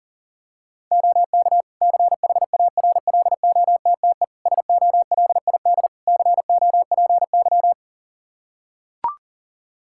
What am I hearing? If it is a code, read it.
Morse code, heard as a keyed tone, about 40 wpm, 700 Hz. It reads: OK CHARLOTTE SOLID COPY
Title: Practice QSO - Exchange 3 - More Info - 40wpm